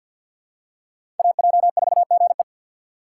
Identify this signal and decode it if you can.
Morse code, heard as a keyed tone, about 53 wpm, 700 Hz. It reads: AJ4GE
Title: USA Call Signs - 5 Characters - 8x Character Spacing - 50wpm